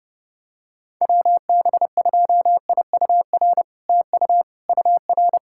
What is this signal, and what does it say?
Morse code, heard as a keyed tone, about 30 wpm, 700 Hz. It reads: WB2IUR TU UR
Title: SOTA QSO Practice 30wpm